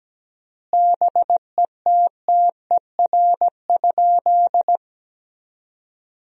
Morse code, heard as a keyed tone, about 17 wpm, 700 Hz. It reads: BETTER?
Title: 3x Word-Spacing - Sentences from Top 400 Words - 17wpm